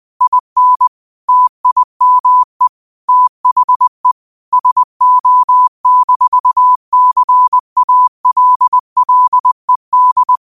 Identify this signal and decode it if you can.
Morse code, heard as a keyed tone, about 20 wpm, 1000 Hz. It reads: IN TIME THE SO-CALLED
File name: 2025-06-25-20wpm